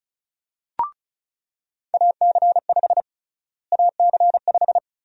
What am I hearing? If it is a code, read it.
Morse code, heard as a keyed tone, about 36 wpm, 700 Hz. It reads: AC5 AC5
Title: USA Call Sign Prefixes - Repeated 2x - 35wpm